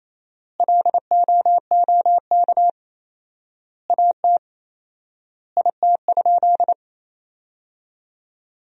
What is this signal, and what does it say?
Morse code, heard as a keyed tone, about 28 wpm, 700 Hz. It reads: LOOK AT IT?
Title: 4x Word-Spacing - Sentences from Top 100 Words - 28wpm